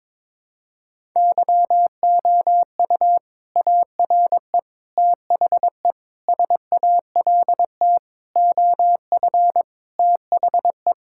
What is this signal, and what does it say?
Morse code, heard as a keyed tone, about 21 wpm, 700 Hz. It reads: YOU ARE THE SALT OF THE
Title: Sentences from Top 1000 Words - Review Part 5 - 22wpm